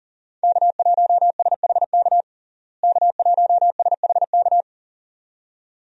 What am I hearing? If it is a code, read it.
Morse code, heard as a keyed tone, about 37 wpm, 700 Hz. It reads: K1SHK K1SHK
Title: USA Call Signs - 5 Characters - Repeated 2x - 40wpm